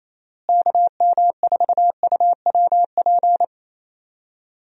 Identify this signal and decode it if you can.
Morse code, heard as a keyed tone, about 28 wpm, 700 Hz. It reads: KM4UWP
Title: USA Call Signs - 6 Characters - Repeated 3x - 28wpm